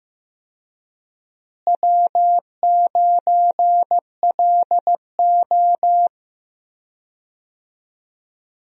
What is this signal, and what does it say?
Morse code, heard as a keyed tone, about 15 wpm, 700 Hz. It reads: W9LO
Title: USA Call Signs - 4 Characters - Repeated 2x - 15wpm